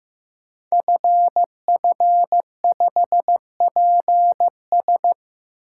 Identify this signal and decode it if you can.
Morse code, heard as a keyed tone, about 15 wpm, 700 Hz. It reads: FF5PS
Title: Code Groups - 5 Letter:Numbers - 15wpm